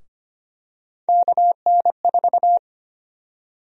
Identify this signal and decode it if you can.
Morse code, heard as a keyed tone, about 25 wpm, 700 Hz. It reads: KN4
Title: USA Call Sign Prefixes - Repeated 2x - 25wpm